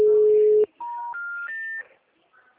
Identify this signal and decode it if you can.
Special information tone.